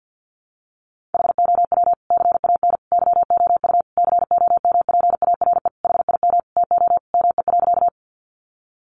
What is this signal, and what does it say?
Morse code, heard as a keyed tone, about 50 wpm, 700 Hz. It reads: HOW CAN YOU COMPARE HIM TO ME.
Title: Sentences from Top 1300 Words 50wpm